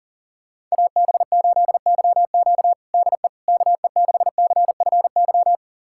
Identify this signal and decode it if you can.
Morse code, heard as a keyed tone, about 40 wpm, 700 Hz. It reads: AB8YQ DE KE6CFY
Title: USA Callsign DE Callsign 40wpm